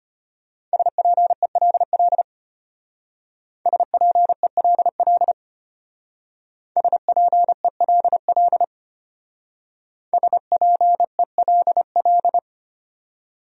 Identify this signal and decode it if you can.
Morse code, heard as a keyed tone, about 29 wpm, 700 Hz. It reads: SPELL SPELL SPELL SPELL
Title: Speed-Racing Top 500 Words 25wpm